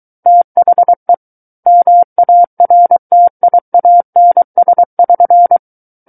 Morse code, heard as a keyed tone, about 23 wpm, 700 Hz. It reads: THE MARTIANS<SN>
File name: Book One / 14-Chapter 14 - 23 WPM